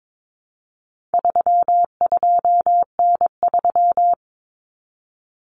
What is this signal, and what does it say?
Morse code, heard as a keyed tone, about 22 wpm, 700 Hz. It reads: 32N3
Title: Common Cut Numbers - 4-Digit 22wpm